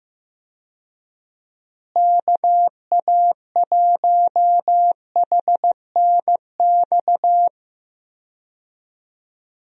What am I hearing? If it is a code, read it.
Morse code, heard as a keyed tone, about 15 wpm, 700 Hz. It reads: KA1HNX
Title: USA Call Signs - 6 Characters - Repeated 2x - 15wpm